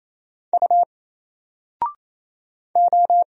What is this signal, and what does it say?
Morse code, heard as a keyed tone, about 27 wpm, 700 Hz. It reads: U O
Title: Single Letter-Number 28wpm